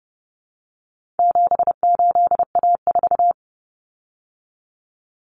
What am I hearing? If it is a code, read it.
Morse code, heard as a keyed tone, about 30 wpm, 700 Hz. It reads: 78A4
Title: Common Cut Numbers - 4-Digit 30wpm